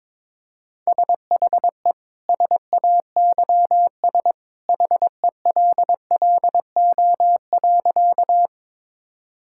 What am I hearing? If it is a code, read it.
Morse code, heard as a keyed tone, about 22 wpm, 700 Hz. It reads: SHE SAYS HELLO.
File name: practice / 22wpm / Sentences with 2-Word Phrases - 22wpm